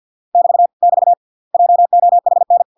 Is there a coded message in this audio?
Yes: Morse code from 0.3 to 2.6 s, heard as a keyed tone, about 50 wpm, 700 Hz. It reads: <BT> <BT> JOHN